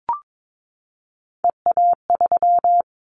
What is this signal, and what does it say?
Morse code, heard as a keyed tone, about 22 wpm, 700 Hz. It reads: EA3
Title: Callsign Prefixes for Top 25 DXCC - 22wpm